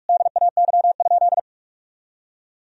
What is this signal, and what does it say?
Morse code, heard as a keyed tone, about 45 wpm, 700 Hz. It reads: DAY?
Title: Sentences from Top 1000 Words - Review Part 2 - 45wpm